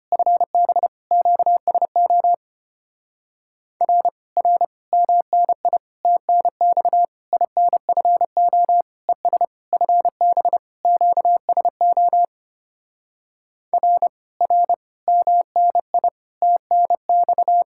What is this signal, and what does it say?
Morse code, heard as a keyed tone, about 29 wpm, 700 Hz. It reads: FB QSO R R MNI TNX INFO ES FB QSO R R MNI TNX
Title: Speed-Racing Common QSO Phrases - 25wpm